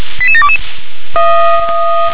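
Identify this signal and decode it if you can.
Five-tone selective call (ZVEI), heard as quick runs of five tones, tones of about 70 ms. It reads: 88722
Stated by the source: ZVEI five-tone selective call received on an SDR.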